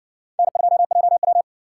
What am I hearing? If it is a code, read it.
Morse code, heard as a keyed tone, about 57 wpm, 700 Hz. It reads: N2JW